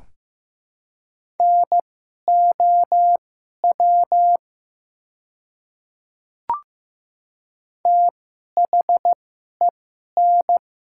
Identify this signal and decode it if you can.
Morse code, heard as a keyed tone, about 15 wpm, spaced as for 11 wpm, 700 Hz. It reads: NOW THEN
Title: 2x Character Spacing - Top 200 Words - 15wpm